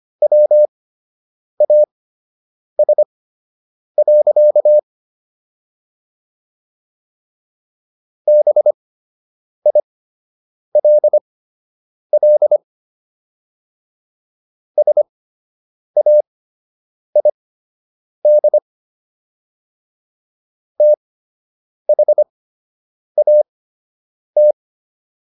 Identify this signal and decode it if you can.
Morse code, heard as a keyed tone, about 25 wpm, spaced as for 8 wpm, 600 Hz. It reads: WAS. BILL SAID THAT